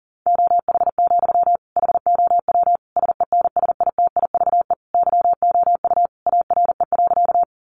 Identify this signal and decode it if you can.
Morse code, heard as a keyed tone, about 40 wpm, 700 Hz. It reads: OH, HOW SENSITIVE YOU ARE.
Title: Sentences from Top 1400 Words 40wpm